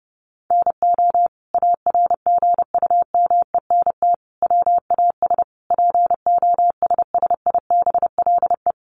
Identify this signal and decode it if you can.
Morse code, heard as a keyed tone, about 28 wpm, 700 Hz. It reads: NO ARGUMENT WAS POSSIBLE
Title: Sentences from Top 1400 Words 30wpm